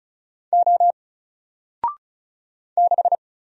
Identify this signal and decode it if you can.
Morse code, heard as a keyed tone, about 35 wpm, 700 Hz. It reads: O 6
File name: Single-Letter-Number-35wpm